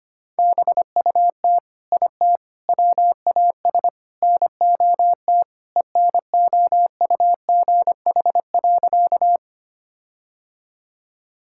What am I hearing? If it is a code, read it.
Morse code, heard as a keyed tone, about 25 wpm, 700 Hz. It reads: BUT IT WAS NOT ENOUGH.
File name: practice / 25wpm / Sentences with 4-Word Phrases - 25wpm